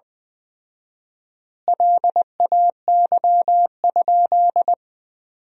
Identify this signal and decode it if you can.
Morse code, heard as a keyed tone, about 20 wpm, 700 Hz. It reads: LAY?
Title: 4x Word-Spacing - Sentences from Top 400 Words - 20wpm